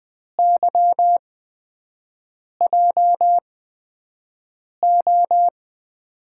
Morse code, heard as a keyed tone, about 20 wpm, spaced as for 5 wpm, 700 Hz. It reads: YJO
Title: USA Call Signs - 8x Character Spacing - 20wpm